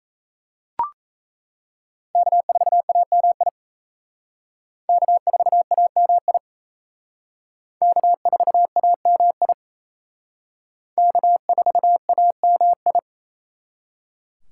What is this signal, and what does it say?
Morse code, heard as a keyed tone, about 32 wpm, 700 Hz. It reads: K4AMI K4AMI K4AMI K4AMI
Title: Speed Racing - USA 1x3 Call Signs 28wpm